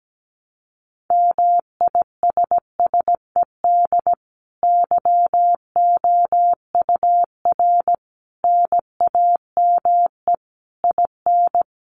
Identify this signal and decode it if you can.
Morse code, heard as a keyed tone, about 17 wpm, 700 Hz. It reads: MISSED YOUR NAME IN
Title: Common QSO Phrases 17wpm